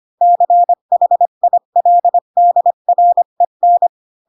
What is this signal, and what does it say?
Morse code, heard as a keyed tone, about 25 wpm, 700 Hz. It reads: CHILDREN
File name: Pinocchio / Pinocchio ch4 20 25 ik1hge